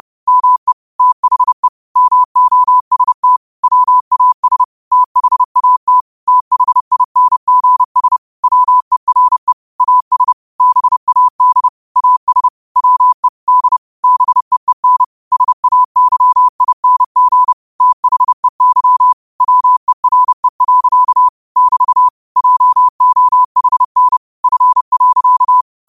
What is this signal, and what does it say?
Morse code, heard as a keyed tone, about 30 wpm, 1000 Hz. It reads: ME THE MOST WAS THAT THINGS WERE AS BAD AS WE D BEEN SAYING THEY WERE. <BT> JOHN F.